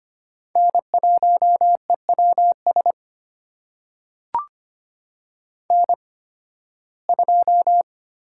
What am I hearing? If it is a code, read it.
Morse code, heard as a keyed tone, about 25 wpm, spaced as for 16 wpm, 700 Hz. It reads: N1EWS N2
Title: USA Call Signs - 5 Characters - 8x Character Spacing - 25wpm